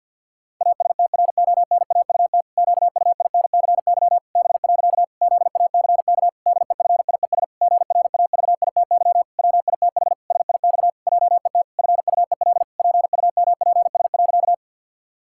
Morse code, heard as a keyed tone, about 49 wpm, 700 Hz. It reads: ASTRONAUT QUINCY B. ZACK DEFIES GRAVITY WITH SIX JET FUEL PUMPS.
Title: Pangram Challenge v2 - 50wpm